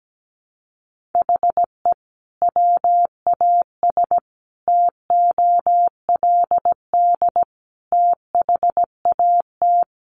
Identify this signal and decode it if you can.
Morse code, heard as a keyed tone, about 17 wpm, 700 Hz. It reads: HE WAS TOLD THAT
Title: Sentences from Top 1500 Words 17wpm